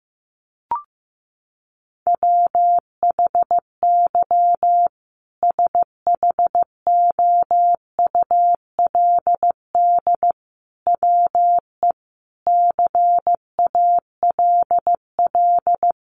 Morse code, heard as a keyed tone, about 14 wpm, 700 Hz. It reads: WHY SHOULD WE CALL
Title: Sentences from Top 1100 Words 15wpm